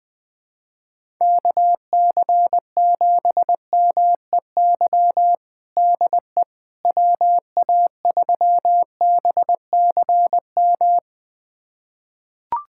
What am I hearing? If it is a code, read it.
Morse code, heard as a keyed tone, about 20 wpm, 700 Hz. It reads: KC7MEY DE WA3BCM
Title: USA Callsign DE Callsign 20wpm